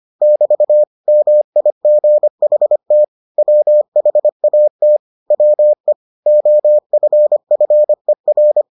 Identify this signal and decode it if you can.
Morse code, heard as a keyed tone, about 25 wpm, 600 Hz. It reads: <BT> MIGHT WHAT WE OFFER